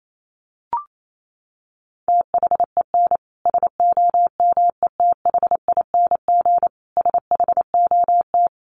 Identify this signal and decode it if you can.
Morse code, heard as a keyed tone, about 28 wpm, 700 Hz. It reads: THEN SOMETHING SHOT
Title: Sentences from Top 1100 Words 28wpm